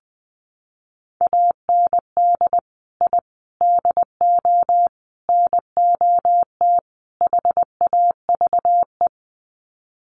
Morse code, heard as a keyed tone, about 20 wpm, 700 Hz. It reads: AND I DO NOT HAVE
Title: Sets of 5 Words Encore 20wpm